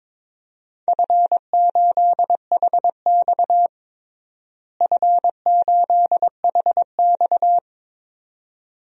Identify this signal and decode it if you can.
Morse code, heard as a keyed tone, about 22 wpm, 700 Hz. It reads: F8HX F8HX
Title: DXCC-100 Call Signs - 4 Characters - Repeated 2x - 22wpm